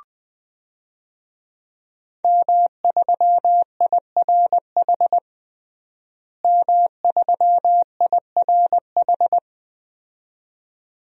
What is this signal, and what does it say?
Morse code, heard as a keyed tone, about 20 wpm, 700 Hz. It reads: M3IRH M3IRH
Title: DXCC-100 Call Signs - 5 Characters - Repeated 2x - 20wpm